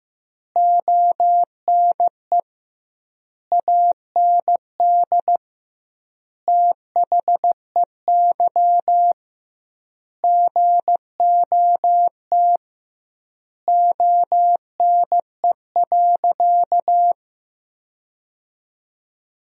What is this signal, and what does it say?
Morse code, heard as a keyed tone, about 15 wpm, 700 Hz. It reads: ONE AND THEY GOT ONE.
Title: 2x Word-Spacing - Sentences from Top 300 Words - 15wpm